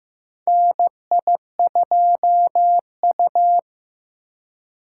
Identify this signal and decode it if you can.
Morse code, heard as a keyed tone, about 16 wpm, 700 Hz. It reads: NI2U
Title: USA Call Signs - 2x Character Spacing - 15wpm